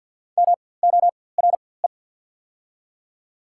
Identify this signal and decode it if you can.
Morse code, heard as a keyed tone, about 47 wpm, spaced as for 23 wpm, 700 Hz. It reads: MORE
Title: 4x Character Spacing - Top 100 Words - 50wpm